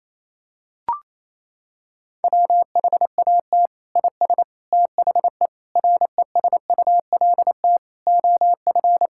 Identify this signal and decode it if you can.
Morse code, heard as a keyed tone, about 28 wpm, 700 Hz. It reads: WHAT IS THE RESULT OF
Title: Sentences from Top 1000 Words - Review Part 5 - 28wpm